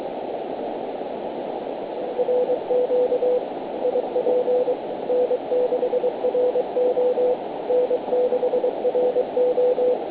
Morse code, heard as a keyed tone, about 22 wpm, 500 Hz, with noise. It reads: RQ IP N6RO N6RO
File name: 160m 0727 gmt